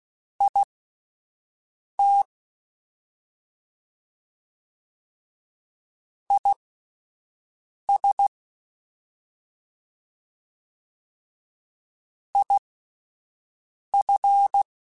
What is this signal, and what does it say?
Morse code, heard as a keyed tone, about 16 wpm, spaced as for 5 wpm, 800 Hz. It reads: IT IS IF
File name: Alphabetical / Alphabetical mp3s / Is @5